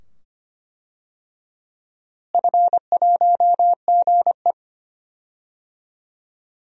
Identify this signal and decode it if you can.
Morse code, heard as a keyed tone, about 25 wpm, 700 Hz. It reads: F1GE
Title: DXCC-100 Call Signs - 4 Characters - Repeated 2x - 25wpm